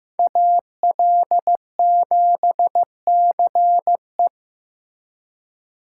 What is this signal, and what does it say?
Morse code, heard as a keyed tone, about 15 wpm, 700 Hz. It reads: AL7CE
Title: USA Call Signs - 5 Characters - 4x Character Spacing - 15wpm